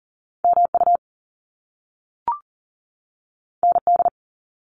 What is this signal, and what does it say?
Morse code, heard as a keyed tone, about 38 wpm, 700 Hz. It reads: MU ND
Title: Top Two Letter Combinations in Words 40wpm